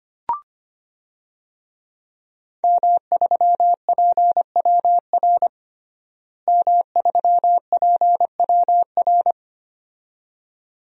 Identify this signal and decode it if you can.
Morse code, heard as a keyed tone, about 25 wpm, 700 Hz. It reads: M3PWR M3PWR
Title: DXCC-100 Call Signs - 5 Characters - Repeated 2x - 25wpm